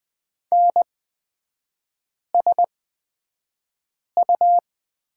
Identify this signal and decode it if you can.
Morse code, heard as a keyed tone, about 20 wpm, spaced as for 5 wpm, 700 Hz. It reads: NSU
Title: 229 Lesson 37e - Letter x - Three Characters 20wpm